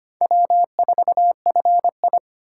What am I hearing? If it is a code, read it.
Morse code, heard as a keyed tone, about 31 wpm, 700 Hz. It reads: W4FI
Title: Speed Racing - USA 1x2 Call Signs 25wpm